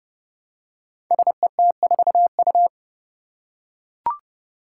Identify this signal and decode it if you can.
Morse code, heard as a keyed tone, about 30 wpm, 700 Hz. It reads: SET4U
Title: Five Challenging Letter-Number Combinations 30wpm